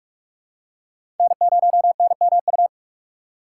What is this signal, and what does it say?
Morse code, heard as a keyed tone, about 46 wpm, 700 Hz. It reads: N0NMU
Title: USA Call Signs - 5 Characters - 4x Character Spacing - 45wpm